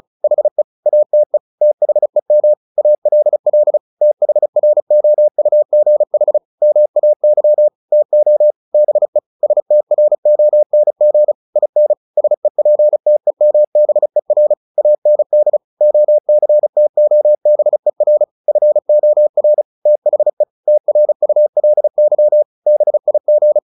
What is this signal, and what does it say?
Morse code, heard as a keyed tone, about 35 wpm, 600 Hz. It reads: HE ATE THEM ALL THROUGH MAY TO BE STRONG IN SEPTEMBER AND OCTOBER FOR THE TRULY BIG
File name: the old man and the sea / the old man and the sea 1 35wpm